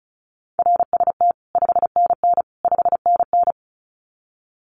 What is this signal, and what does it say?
Morse code, heard as a keyed tone, about 34 wpm, 700 Hz. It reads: RST 5NN 5NN
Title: Common QSO Phrases Encore 35wpm